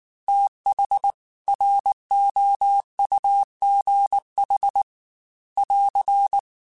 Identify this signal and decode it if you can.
Morse code, heard as a keyed tone, about 19 wpm, 800 Hz. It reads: TH ROUGH <AR>